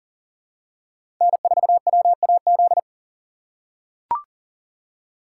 Morse code, heard as a keyed tone, about 40 wpm, 700 Hz. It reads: N4WAZ
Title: USA Call Signs - 5 Characters - 6x Character Spacing - 40wpm